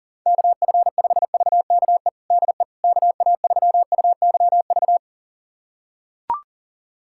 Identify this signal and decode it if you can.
Morse code, heard as a keyed tone, about 37 wpm, 700 Hz. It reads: KF5VKE DE KA3UYV
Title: USA Callsign DE Callsign - 6 Characters - 40wpm